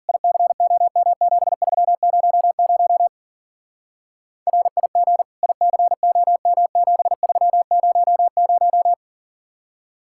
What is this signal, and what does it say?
Morse code, heard as a keyed tone, about 50 wpm, 700 Hz. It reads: ICOM7300 RIG ICOM7300
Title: Speed-Racing Common QSO Phrases - 40wpm